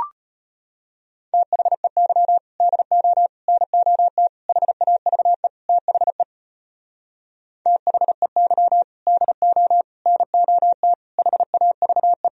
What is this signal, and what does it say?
Morse code, heard as a keyed tone, about 33 wpm, 700 Hz. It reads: THEY DO NOT HAVE THE THEY DO NOT HAVE